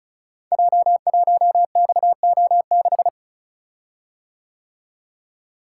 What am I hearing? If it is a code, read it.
Morse code, heard as a keyed tone, about 36 wpm, 700 Hz. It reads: J1XO6
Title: Code Groups - 5 Letter:Numbers - 35wpm